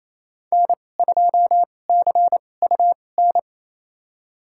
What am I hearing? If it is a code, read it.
Morse code, heard as a keyed tone, about 27 wpm, spaced as for 20 wpm, 700 Hz. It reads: N2CUN
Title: USA Call Signs - 2x Character Spacing - 28wpm